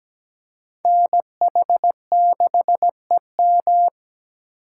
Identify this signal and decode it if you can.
Morse code, heard as a keyed tone, about 20 wpm, 700 Hz. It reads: NH6EM
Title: Speed Racing - USA Call Signs 17wpm